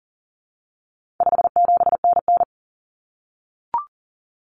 Morse code, heard as a keyed tone, about 40 wpm, 700 Hz. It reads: 57NN
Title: Common Cut Numbers - 4-Digit 40wpm